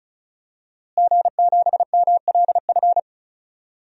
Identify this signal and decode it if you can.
Morse code, heard as a keyed tone, about 34 wpm, 700 Hz. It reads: G7MLF
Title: DXCC-100 Call Signs - 5 Characters - Repeated 3x - 35wpm